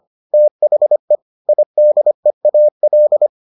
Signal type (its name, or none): Morse code